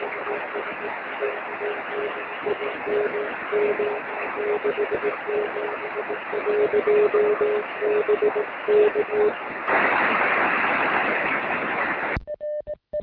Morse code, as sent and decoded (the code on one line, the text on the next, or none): none